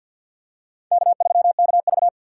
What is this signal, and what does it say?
Morse code, heard as a keyed tone, about 47 wpm, 700 Hz. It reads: K3KV
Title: USA Call Signs - 4 Characters - 3x Character Spacing - 50wpm